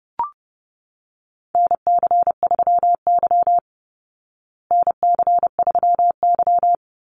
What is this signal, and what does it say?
Morse code, heard as a keyed tone, about 30 wpm, 700 Hz. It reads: NC3Y NC3Y
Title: Call Signs Repeated 3x - 30wpm